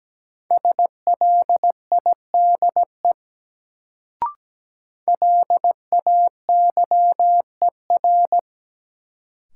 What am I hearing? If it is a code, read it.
Morse code, heard as a keyed tone, about 17 wpm, 700 Hz. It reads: SLIDE LAYER
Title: Sentences from Top 1600 Words - New Word Drill Alt - 17wpm